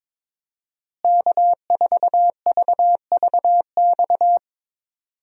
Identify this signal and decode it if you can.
Morse code, heard as a keyed tone, about 22 wpm, 700 Hz. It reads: K4VVX
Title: USA Call Signs - 5 Characters - Repeated 3x - 22wpm